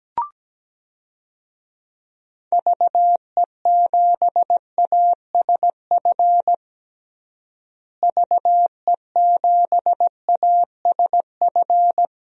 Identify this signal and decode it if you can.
Morse code, heard as a keyed tone, about 17 wpm, 700 Hz. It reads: VE7ASF VE7ASF
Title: DXCC-100 Call Signs - 6 Characters - Repeated 3x - 17wpm